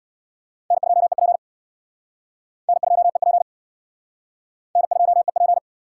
Morse code, heard as a keyed tone, about 62 wpm, 700 Hz. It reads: N2EP N2EP N2EP
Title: Speed Racing - USA 1x2 Call Signs 50wpm